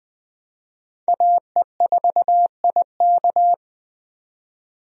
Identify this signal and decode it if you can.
Morse code, heard as a keyed tone, about 20 wpm, 700 Hz. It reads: AE4IK
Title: USA Call Signs - 5 Characters - 20wpm